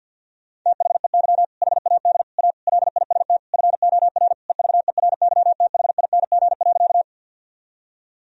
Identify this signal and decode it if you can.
Morse code, heard as a keyed tone, about 50 wpm, 700 Hz. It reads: THEY HAD A LIST FOR EVERYTHING.
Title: Sentences from Top 1000 Words - Review Part 2 - 50wpm